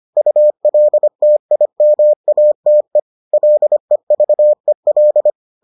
Morse code, heard as a keyed tone, about 25 wpm, 600 Hz. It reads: ULTIMATE LEVEL